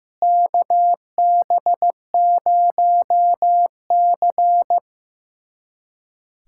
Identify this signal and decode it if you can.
Morse code, heard as a keyed tone, about 15 wpm, 700 Hz. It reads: KB0C
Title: USA Callsign DE Callsign - 4 Characters - 15wpm